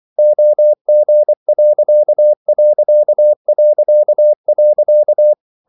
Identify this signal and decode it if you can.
Morse code, heard as a keyed tone, about 24 wpm, 600 Hz. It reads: OG....
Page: Ferie 24 wpm